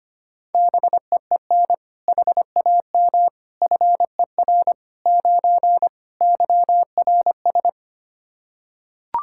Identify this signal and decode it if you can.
Morse code, heard as a keyed tone, about 25 wpm, 700 Hz. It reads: BEEN HAM FER 9 YRS
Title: Practice QSO - Exchange 3 - More Info - 25wpm